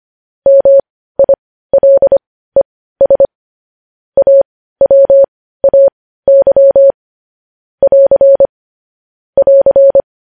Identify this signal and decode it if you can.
Morse code, heard as a keyed tone, about 25 wpm, spaced as for 15 wpm, 550 Hz. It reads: MILES AWAY <AR> <AR>
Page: news-headlines-in-morse-code-at-15-wpm